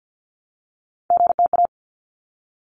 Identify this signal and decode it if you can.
Morse code, heard as a keyed tone, about 50 wpm, 700 Hz. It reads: GTA
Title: ARRL Sections 50wpm